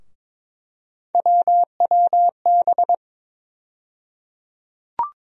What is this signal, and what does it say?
Morse code, heard as a keyed tone, about 22 wpm, 700 Hz. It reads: WWB
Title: USA Call Sign Postfixes - 3 Letters 22wpm